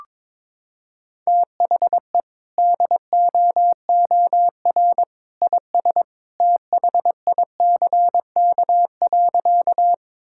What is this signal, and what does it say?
Morse code, heard as a keyed tone, about 22 wpm, 700 Hz. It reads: THE DOOR IS THICK.
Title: Sentences from Top 1600 Words 22wpm